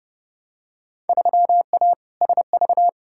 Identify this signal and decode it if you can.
Morse code, heard as a keyed tone, about 31 wpm, 700 Hz. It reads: 3A SV
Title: Field Day Exchanges - 30wpm